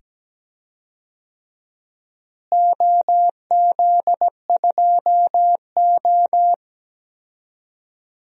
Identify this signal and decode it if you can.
Morse code, heard as a keyed tone, about 17 wpm, 700 Hz. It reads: OZ2O
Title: DXCC-100 Call Signs - 4 Characters - Repeated 2x - 17wpm